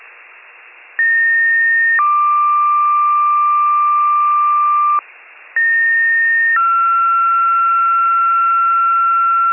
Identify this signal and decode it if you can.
Two-tone page, heard as a one-second tone, then a three-second tone, about 1850 then 1180 Hz, about 1850 then 1360 Hz, with noise. It is heard from 1.0 to 9.5 s.